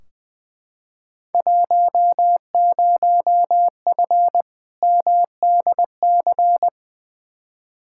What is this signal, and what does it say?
Morse code, heard as a keyed tone, about 20 wpm, 700 Hz. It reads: 10F MDC
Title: Field Day Exchanges Encore - 20wpm